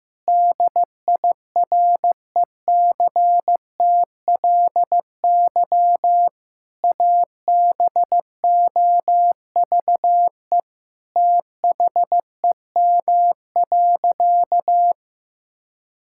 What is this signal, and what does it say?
Morse code, heard as a keyed tone, about 15 wpm, 700 Hz. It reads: DIRECTLY ABOVE THEM.
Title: Sentences from Top 1000 Words - Review Part 5 - 15wpm